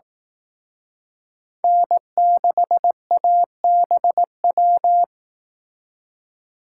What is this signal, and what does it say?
Morse code, heard as a keyed tone, about 18 wpm, 700 Hz. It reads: N6ABW
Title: Speed Racing - USA 1x3 Call Signs 15wpm